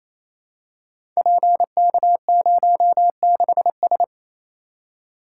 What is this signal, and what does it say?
Morse code, heard as a keyed tone, about 28 wpm, 700 Hz. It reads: PK06S
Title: Code Groups - 5 Letter:Numbers - 28wpm